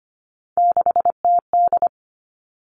Morse code, heard as a keyed tone, about 25 wpm, 700 Hz. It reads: 6TD
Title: All Cut Numbers - 3-Digit 25wpm